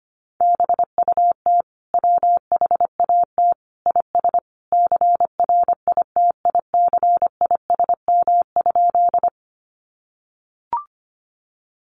Morse code, heard as a keyed tone, about 25 wpm, 700 Hz. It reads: BUT WHAT IS CRITICISM?
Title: Sentences from Top 1300 Words 25wpm